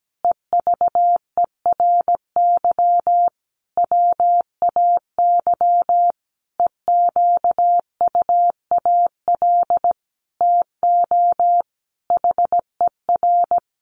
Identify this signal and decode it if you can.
Morse code, heard as a keyed tone, about 17 wpm, 700 Hz. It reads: EVERY WAY EQUAL TO HER
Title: Sentences from Top 1400 Words 17wpm